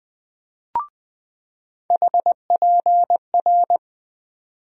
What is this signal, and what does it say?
Morse code, heard as a keyed tone, about 20 wpm, 700 Hz. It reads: HPR
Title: USA Call Sign Postfixes - 3 Letters 20wpm